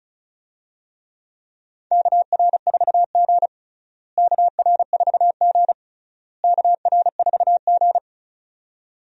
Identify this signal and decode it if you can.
Morse code, heard as a keyed tone, about 36 wpm, 700 Hz. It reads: KR4G KR4G KR4G
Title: USA Call Signs - 4 Characters - Repeated 3x - 35wpm